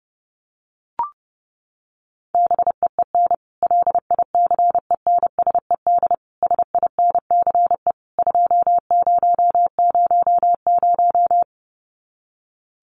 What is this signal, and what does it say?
Morse code, heard as a keyed tone, about 30 wpm, 700 Hz. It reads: BEEN LICENSED SINCE 2000
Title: Common QSO Phrases 30wpm